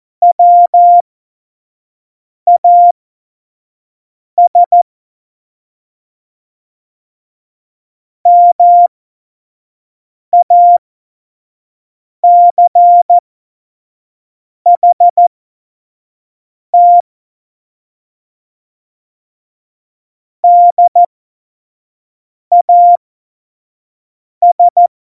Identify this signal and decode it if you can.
Morse code, heard as a keyed tone, about 14 wpm, spaced as for 5 wpm, 700 Hz. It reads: WAS MACHT DAS
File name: Morsekurs / Witze / Witz 27. August 2019, 25 F70